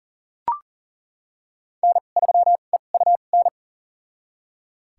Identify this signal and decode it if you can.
Morse code, heard as a keyed tone, about 37 wpm, spaced as for 28 wpm, 700 Hz. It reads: N3EUN
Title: USA Call Signs - 2x Character Spacing - 40wpm